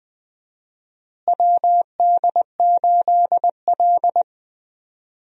Morse code, heard as a keyed tone, about 20 wpm, 700 Hz. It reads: WD8L
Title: USA Call Signs - 4 Characters - Repeated 2x - 20wpm